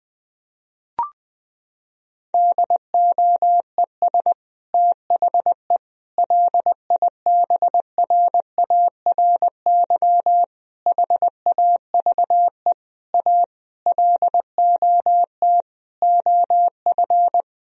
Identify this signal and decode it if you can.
Morse code, heard as a keyed tone, about 21 wpm, 700 Hz. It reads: DOES THE LIBRARY HAVE A LOT OF
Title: Sentences from Top 1000 Words - Review Part 4 - 20wpm